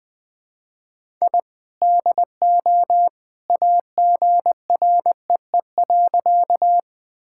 Morse code, heard as a keyed tone, about 20 wpm, 700 Hz. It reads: I DO AGREE.
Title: Sentences with 2-Word Phrases - 20wpm